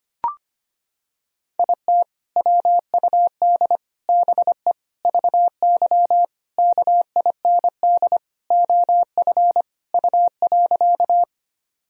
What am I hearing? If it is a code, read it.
Morse code, heard as a keyed tone, about 25 wpm, 700 Hz. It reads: IT WUD BE VY KIND OF U.
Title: Sentences from Top 200 Words and Abbreviations 25wpm